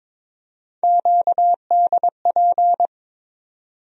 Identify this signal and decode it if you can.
Morse code, heard as a keyed tone, about 22 wpm, 700 Hz. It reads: QDP
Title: USA Call Sign Postfixes - 3 Letters 22wpm